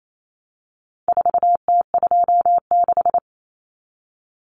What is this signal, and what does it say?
Morse code, heard as a keyed tone, about 28 wpm, 700 Hz. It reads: 4T26
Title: Common Cut Numbers - 4-Digit 28wpm